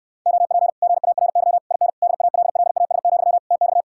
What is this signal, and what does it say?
Morse code, heard as a keyed tone, about 69 wpm, 700 Hz. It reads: OP GARY IN DALLAS, TX